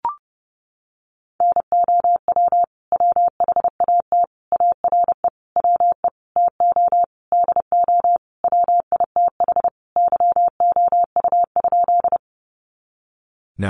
Morse code, heard as a keyed tone, about 30 wpm, 700 Hz. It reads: NOW WHAT ARE WE TO DO WITH YOU?
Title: Sentences from Top 100 Words 30wpm